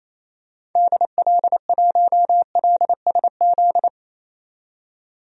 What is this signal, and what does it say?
Morse code, heard as a keyed tone, about 28 wpm, 700 Hz. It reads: DL1LSZ
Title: DXCC-100 Call Signs - 6 Characters - 28wpm